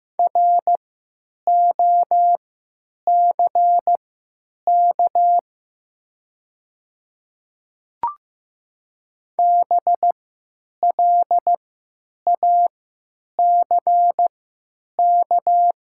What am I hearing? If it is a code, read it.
Morse code, heard as a keyed tone, about 15 wpm, spaced as for 9 wpm, 700 Hz. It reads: ROCK BLACK
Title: 3x Character Spacing - Top 500 Words - 15wpm